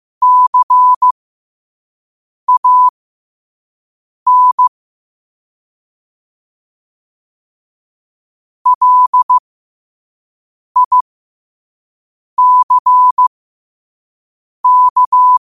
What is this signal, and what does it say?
Morse code, heard as a keyed tone, about 15 wpm, spaced as for 5 wpm, 1000 Hz. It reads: CAN LICK
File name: qotdfiles / 5wpm